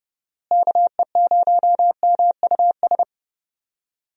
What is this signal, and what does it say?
Morse code, heard as a keyed tone, about 29 wpm, 700 Hz. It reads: KE0MUS
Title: USA Call Signs - 6 Characters - Repeated 3x - 30wpm